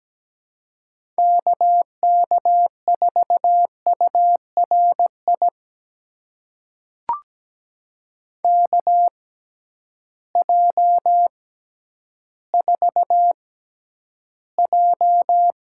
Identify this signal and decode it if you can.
Morse code, heard as a keyed tone, about 17 wpm, spaced as for 10 wpm, 700 Hz. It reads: KK4URI KJ4J
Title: USA Call Signs - 6x Character Spacing - 17wpm